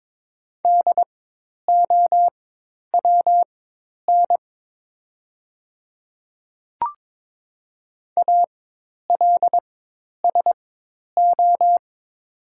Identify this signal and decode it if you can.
Morse code, heard as a keyed tone, about 22 wpm, spaced as for 10 wpm, 690 Hz. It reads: DOWN ALSO
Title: 4x Character Spacing - Top 200 Words - 22wpm